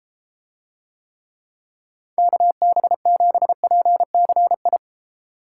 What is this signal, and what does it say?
Morse code, heard as a keyed tone, about 32 wpm, 700 Hz. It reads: KB7PCI